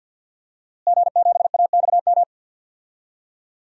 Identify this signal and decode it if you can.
Morse code, heard as a keyed tone, about 51 wpm, 700 Hz. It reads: G7AXM
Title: DXCC-100 Call Signs - 5 Characters - Repeated 2x - 50wpm